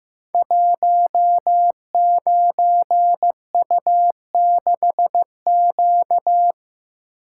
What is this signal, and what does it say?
Morse code, heard as a keyed tone, about 15 wpm, 700 Hz. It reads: 19U6Q
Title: Code Groups - 5 Letter:Numbers - 15wpm